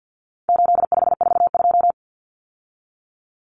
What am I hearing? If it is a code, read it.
Morse code, heard as a keyed tone, about 50 wpm, 700 Hz. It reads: 8542
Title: Common Cut Numbers - 4-Digit 50wpm